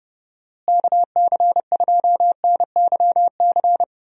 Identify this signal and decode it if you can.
Morse code, heard as a keyed tone, about 31 wpm, 700 Hz. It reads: KC2NYC
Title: USA Call Signs - 6 Characters - Repeated 3x - 30wpm